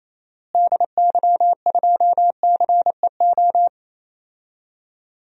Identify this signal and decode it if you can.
Morse code, heard as a keyed tone, about 28 wpm, 700 Hz. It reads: DY2CEO
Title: DXCC-100 Call Signs - 6 Characters - Repeated 3x - 28wpm